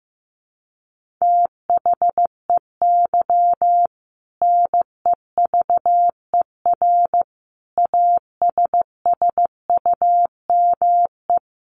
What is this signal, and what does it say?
Morse code, heard as a keyed tone, about 15 wpm, 700 Hz. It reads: THEY NEVER ASSUME
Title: Sentences from Top 1500 Words 15wpm